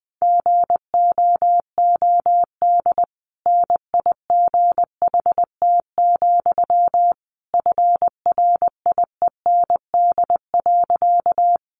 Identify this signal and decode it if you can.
Morse code, heard as a keyed tone, about 19 wpm, 700 Hz. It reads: GOOD NIGHT, FRIEND.
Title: Sentences from Top 300 Words 20wpm